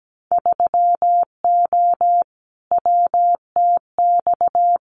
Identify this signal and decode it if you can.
Morse code, heard as a keyed tone, about 17 wpm, 700 Hz. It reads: 3O WTX
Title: Winter Field Day Exchanges Encore 17wpm